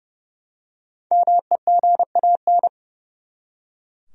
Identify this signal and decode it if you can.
Morse code, heard as a keyed tone, about 30 wpm, 700 Hz. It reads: MEGAN
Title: USA Common Names - 30wpm